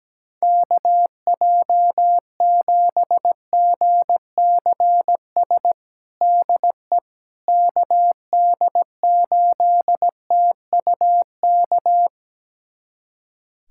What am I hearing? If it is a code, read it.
Morse code, heard as a keyed tone, about 17 wpm, 700 Hz. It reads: KJ7GCS DE KD8TUK
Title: USA Callsign DE Callsign 17wpm